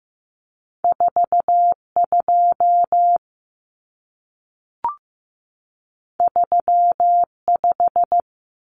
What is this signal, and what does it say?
Morse code, heard as a keyed tone, about 15 wpm, 700 Hz. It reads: 42 35
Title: Numbers - 2-Digit 15wpm